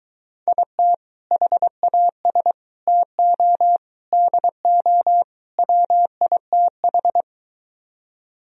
Morse code, heard as a keyed tone, about 23 wpm, 700 Hz. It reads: IT HAS TO DO WITH